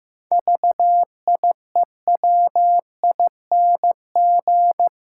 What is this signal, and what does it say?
Morse code, heard as a keyed tone, about 15 wpm, 700 Hz. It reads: VIEWING
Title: Pangram Challenge v2 - 15wpm